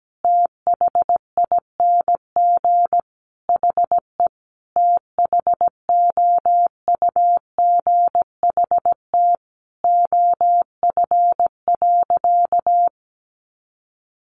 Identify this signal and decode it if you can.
Morse code, heard as a keyed tone, about 17 wpm, 700 Hz. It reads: THING HE THOUGHT OF.
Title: Sentences from Top 300 Words 17wpm